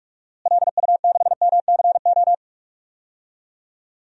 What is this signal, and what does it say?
Morse code, heard as a keyed tone, about 47 wpm, 700 Hz. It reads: LU6MCO
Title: DXCC-100 Call Signs - 6 Characters - Repeated 2x - 45wpm